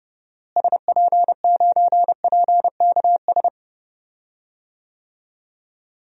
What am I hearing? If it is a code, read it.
Morse code, heard as a keyed tone, about 30 wpm, 700 Hz. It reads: SP9PKS
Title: DXCC-100 Call Signs - 6 Characters - Repeated 2x - 30wpm